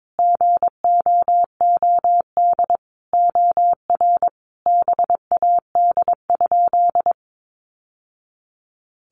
Morse code, heard as a keyed tone, about 22 wpm, 700 Hz. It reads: GOOD OR BAD?
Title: Sentences from Top 800 Words 22wpm